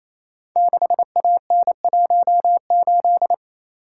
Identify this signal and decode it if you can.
Morse code, heard as a keyed tone, about 27 wpm, 700 Hz. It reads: 6AN18
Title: Code Groups - 5 Letter:Numbers - 28wpm